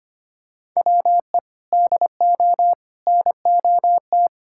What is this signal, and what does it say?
Morse code, heard as a keyed tone, about 25 wpm, 700 Hz. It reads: WE DO NOT
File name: practice / 25wpm / Sentences with 4-Word Phrases - 25wpm